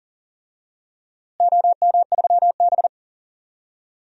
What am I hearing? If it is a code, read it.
Morse code, heard as a keyed tone, about 40 wpm, 700 Hz. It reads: OM3B
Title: DXCC-100 Call Signs - 4 Characters - Repeated 3x - 40wpm